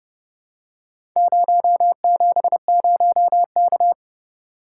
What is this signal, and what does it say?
Morse code, heard as a keyed tone, about 31 wpm, 700 Hz. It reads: 070K